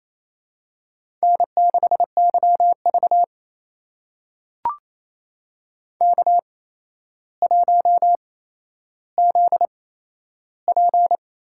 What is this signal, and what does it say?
Morse code, heard as a keyed tone, about 28 wpm, spaced as for 12 wpm, 700 Hz. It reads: N6YV K1ZP
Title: USA Call Signs - 4 Characters - 8x Character Spacing - 28wpm